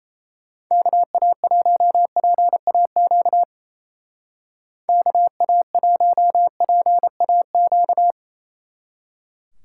Morse code, heard as a keyed tone, about 34 wpm, 700 Hz. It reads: KA1PAQ KA1PAQ
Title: Speed Racing - USA 2x3 Call Signs 28wpm